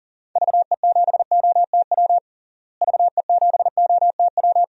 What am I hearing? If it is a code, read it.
Morse code, heard as a keyed tone, about 40 wpm, 700 Hz. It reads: VE7OTW VE7OTW
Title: DXCC-100 Call Signs - 6 Characters - Repeated 3x - 40wpm